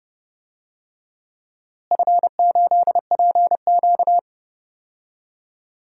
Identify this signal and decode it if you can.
Morse code, heard as a keyed tone, about 30 wpm, 700 Hz. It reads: F8PQ
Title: DXCC-100 Call Signs - 4 Characters - Repeated 3x - 30wpm